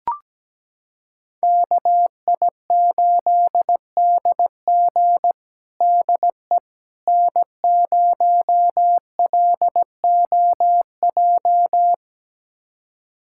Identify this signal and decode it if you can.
Morse code, heard as a keyed tone, about 17 wpm, 700 Hz. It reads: KI8DG DE N0LOJ
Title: USA Callsign DE Callsign - 5 Characters - 17wpm